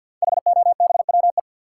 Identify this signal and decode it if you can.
Morse code, heard as a keyed tone, about 47 wpm, 700 Hz. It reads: HOBWE
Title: Code Groups - 5 Letters - 50wpm